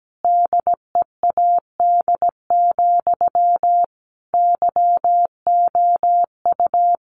Morse code, heard as a keyed tone, about 17 wpm, 700 Hz. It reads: DEAD, YOU
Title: A Sound Of Thunder 17wpm